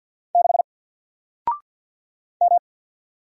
Morse code, heard as a keyed tone, about 48 wpm, 700 Hz. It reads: 6 M
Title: Single Character (Letter:Number:Punctuation:Prosign) 50wpm